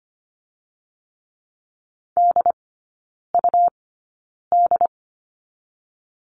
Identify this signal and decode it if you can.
Morse code, heard as a keyed tone, about 25 wpm, 700 Hz. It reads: D U D
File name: course / 25:10wpm / 100 Lesson 17e - Letter u - Three Characters 25wpm